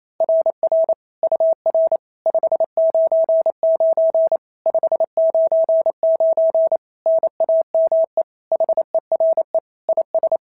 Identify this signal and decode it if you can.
Morse code, heard as a keyed tone, about 28 wpm, 690 Hz. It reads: RR UR 599 599 NAME HERE IS
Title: SOTA QSO Practice 28wpm